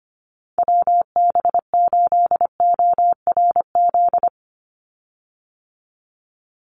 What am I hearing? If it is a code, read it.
Morse code, heard as a keyed tone, about 25 wpm, 700 Hz. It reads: WB8ORZ
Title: USA Call Signs 25wpm